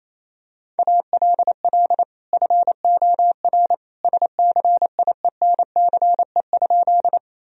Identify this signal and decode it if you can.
Morse code, heard as a keyed tone, about 28 wpm, 700 Hz. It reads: ALL FOR SCIENCE?
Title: Sentences from Top 1000 Words - Review Part 2 - 28wpm